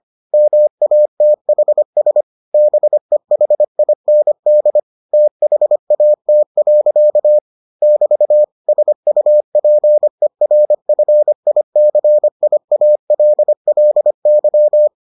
Morse code, heard as a keyed tone, about 25 wpm, 600 Hz. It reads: MATHS BEHIND THAT. <BT> SUPERFICIALLY